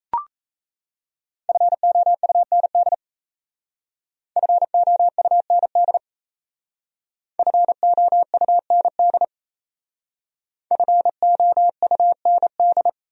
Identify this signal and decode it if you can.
Morse code, heard as a keyed tone, about 35 wpm, 700 Hz. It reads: FOUND FOUND FOUND FOUND